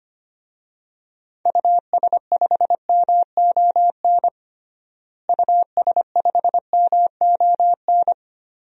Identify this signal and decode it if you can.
Morse code, heard as a keyed tone, about 25 wpm, 700 Hz. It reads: US5MON US5MON
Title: DXCC-100 Call Signs - 6 Characters - Repeated 3x - 25wpm